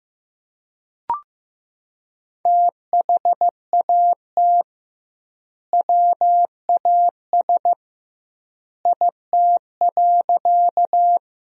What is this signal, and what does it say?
Morse code, heard as a keyed tone, about 15 wpm, 700 Hz. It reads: THAT WAS IT.
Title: 2x Word-Spacing - Sentences from Top 100 Words - 15wpm